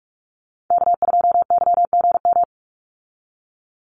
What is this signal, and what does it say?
Morse code, heard as a keyed tone, about 44 wpm, 700 Hz. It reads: K2YGM
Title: Call Signs - 45wpm